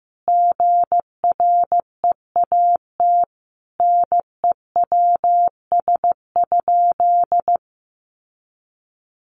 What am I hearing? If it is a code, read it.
Morse code, heard as a keyed tone, about 15 wpm, 700 Hz. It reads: GREAT NEWS?
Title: Sentences from Top 800 Words 15wpm